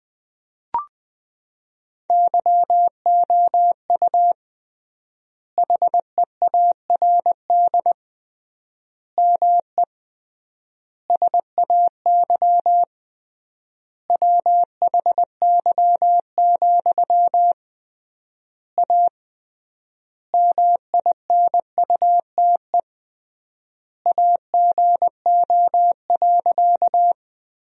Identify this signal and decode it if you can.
Morse code, heard as a keyed tone, about 20 wpm, 700 Hz. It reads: YOU HEARD ME SAY WHY, A MINUTE AGO.
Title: 3x Word-Spacing - Sentences from Top 500 Words - 20wpm